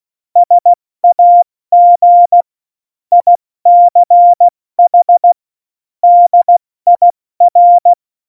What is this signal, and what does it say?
Morse code, heard as a keyed tone, about 16 wpm, 700 Hz. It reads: SAG ICH DIR
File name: Morsekurs / Witze / Witz 29. Mai 2018, 70 F80